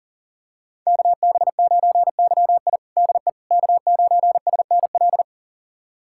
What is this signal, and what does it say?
Morse code, heard as a keyed tone, about 37 wpm, 700 Hz. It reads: KB9YI DE K9SNL
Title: USA Callsign DE Callsign - 5 Characters - 40wpm